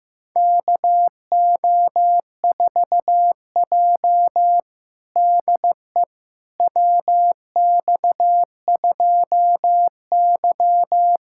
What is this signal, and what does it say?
Morse code, heard as a keyed tone, about 15 wpm, 700 Hz. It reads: KO4J DE WX2Y
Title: USA Callsign DE Callsign - 4 Characters - 15wpm